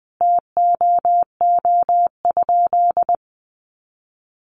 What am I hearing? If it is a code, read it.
Morse code, heard as a keyed tone, about 20 wpm, 700 Hz. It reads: TOO?
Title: Sentences from Top 500 Words 20wpm